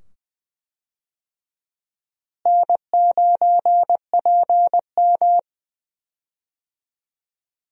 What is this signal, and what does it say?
Morse code, heard as a keyed tone, about 20 wpm, 700 Hz. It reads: N9PM
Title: USA Call Signs - 4 Characters - Repeated 2x - 20wpm